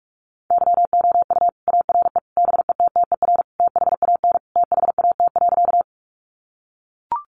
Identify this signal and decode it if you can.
Morse code, heard as a keyed tone, about 44 wpm, 700 Hz. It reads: YOU ARE BETTER THAN THAT.
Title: Sentences from Top 400 Words 45wpm